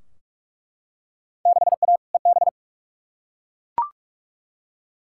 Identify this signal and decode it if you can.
Morse code, heard as a keyed tone, about 42 wpm, 700 Hz. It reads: 6A EB
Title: Field Day Exchanges Encore - 45wpm